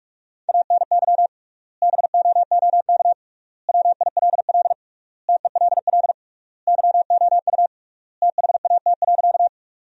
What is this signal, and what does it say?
Morse code, heard as a keyed tone, about 46 wpm, 700 Hz. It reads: ANY BOOK WILL TELL YOU THAT.
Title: 3x Word-Spacing - Sentences from Top 300 Words - 45wpm